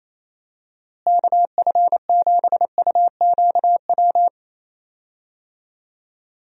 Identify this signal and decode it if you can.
Morse code, heard as a keyed tone, about 28 wpm, 700 Hz. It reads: KF7UQW
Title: USA Call Signs - 6 Characters - Repeated 2x - 28wpm